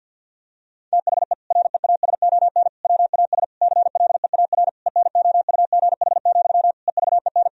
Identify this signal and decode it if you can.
Morse code, heard as a keyed tone, about 50 wpm, 700 Hz. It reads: THE REASON WAS CLEAR ENOUGH, EVEN
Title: Sentences from Top 1000 Words - Review Part 2 - 50wpm